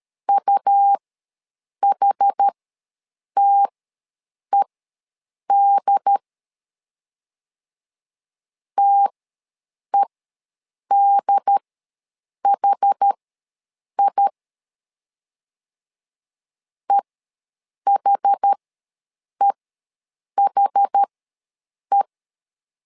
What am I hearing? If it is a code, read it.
Morse code, heard as a keyed tone, about 13 wpm, spaced as for 7 wpm, 800 Hz. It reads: UHTED TEDHI EHEHE
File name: LEZIONI / 2 UHDN / 2 UHD ws 12wpm 1722